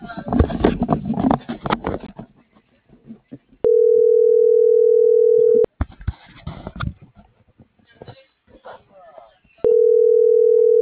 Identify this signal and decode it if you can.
Telephone call progress tone (North America), heard as a telephone network's tone, about 440, 480 Hz, with speech. Ringback tone.